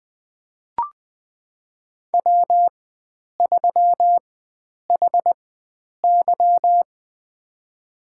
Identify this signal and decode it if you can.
Morse code, heard as a keyed tone, about 21 wpm, spaced as for 9 wpm, 700 Hz. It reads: W3HY
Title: USA Call Signs - 4 Characters - 4x Character Spacing - 20wpm